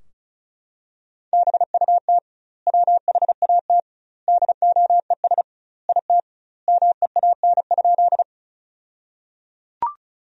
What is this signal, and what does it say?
Morse code, heard as a keyed tone, about 34 wpm, 700 Hz. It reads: BUT WHAT DOES IT MEAN?
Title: 2x Word-Spacing - Sentences from Top 200 Words - 35wpm